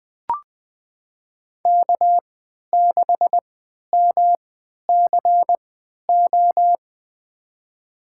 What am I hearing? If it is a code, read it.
Morse code, heard as a keyed tone, about 21 wpm, spaced as for 11 wpm, 700 Hz. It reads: K6MCO
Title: USA Call Signs - 5 Characters - 3x Character Spacing - 20wpm